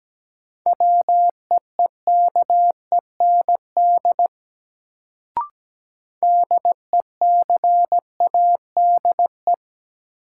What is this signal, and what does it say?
Morse code, heard as a keyed tone, about 17 wpm, 700 Hz. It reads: WEEKEND DECADE
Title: Sets of 5 Words - New Word Drill - 17wpm